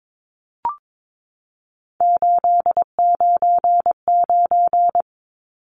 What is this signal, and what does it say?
Morse code, heard as a keyed tone, about 22 wpm, 700 Hz. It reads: 899
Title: Numbers - 3-Digit 22wpm